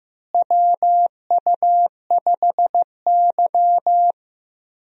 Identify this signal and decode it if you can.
Morse code, heard as a keyed tone, about 15 wpm, 700 Hz. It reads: WU5Y
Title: USA Call Signs - 4 Characters - Repeated 3x - 15wpm